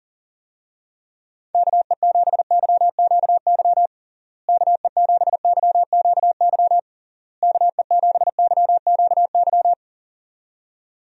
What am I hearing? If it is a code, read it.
Morse code, heard as a keyed tone, about 40 wpm, 700 Hz. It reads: KE7YQY KE7YQY KE7YQY
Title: USA Call Signs - 6 Characters - Repeated 3x - 40wpm